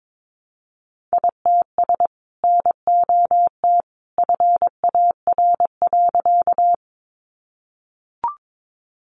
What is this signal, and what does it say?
Morse code, heard as a keyed tone, about 22 wpm, 700 Hz. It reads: ITS NOT FAR.
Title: Sentences from Top 600 Words 22wpm